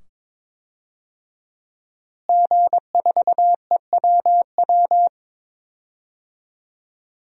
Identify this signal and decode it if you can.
Morse code, heard as a keyed tone, about 22 wpm, 700 Hz. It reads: G4EWW
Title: DXCC-100 Call Signs - 5 Characters - Repeated 2x - 22wpm